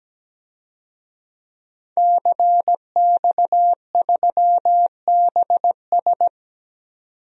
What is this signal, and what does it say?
Morse code, heard as a keyed tone, about 17 wpm, 700 Hz. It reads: CX3BS
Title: DXCC-100 Call Signs - 5 Characters - 17wpm